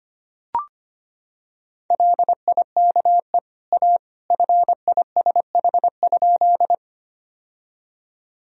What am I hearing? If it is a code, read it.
Morse code, heard as a keyed tone, about 25 wpm, 700 Hz. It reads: LIKE A FISH?
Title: Sentences from Top 1000 Words - Review Part 1 - 25wpm